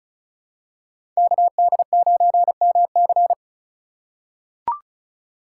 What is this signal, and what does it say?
Morse code, heard as a keyed tone, about 36 wpm, 700 Hz. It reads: KD9MC
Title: USA Call Signs - 5 Characters - 2x Character Spacing - 35wpm